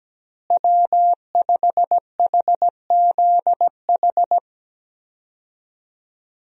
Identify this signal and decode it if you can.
Morse code, heard as a keyed tone, about 17 wpm, 700 Hz. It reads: W5HZH
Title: USA Call Signs - 5 Characters - Repeated 3x - 17wpm